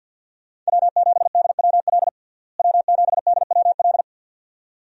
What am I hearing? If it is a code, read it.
Morse code, heard as a keyed tone, about 46 wpm, 700 Hz. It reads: W7DWL W7DWL
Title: USA Call Signs - 5 Characters - Repeated 2x - 50wpm